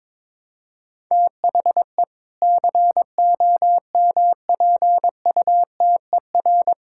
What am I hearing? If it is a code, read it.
Morse code, heard as a keyed tone, about 22 wpm, 700 Hz. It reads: THE COMPUTER
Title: Sentences from Top 1000 Words - Review Part 6 - 22wpm